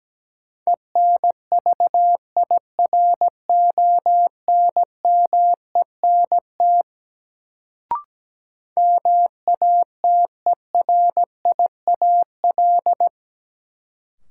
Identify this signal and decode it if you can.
Morse code, heard as a keyed tone, about 17 wpm, 700 Hz. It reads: ENVIRONMENT MATERIAL
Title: Sentences from Top 1000 Words - Review Part 5 - New Word Drill - 17wpm